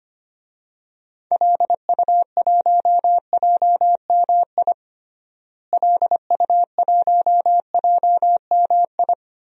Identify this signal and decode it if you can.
Morse code, heard as a keyed tone, about 25 wpm, 700 Hz. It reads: LU1JMI LU1JMI
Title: DXCC-100 Call Signs - 6 Characters - Repeated 2x - 25wpm